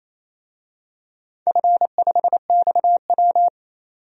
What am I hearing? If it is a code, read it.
Morse code, heard as a keyed tone, about 28 wpm, 700 Hz. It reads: F5XW
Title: DXCC-100 Call Signs - 4 Characters - 28wpm